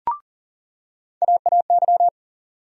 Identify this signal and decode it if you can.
Morse code, heard as a keyed tone, about 40 wpm, 700 Hz. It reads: AAY
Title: USA Call Sign Postfixes - 3 Letters 40wpm